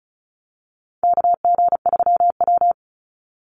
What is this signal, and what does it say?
Morse code, heard as a keyed tone, about 34 wpm, 700 Hz. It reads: KG3W
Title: NA SOTA Chaser Callsigns 35wpm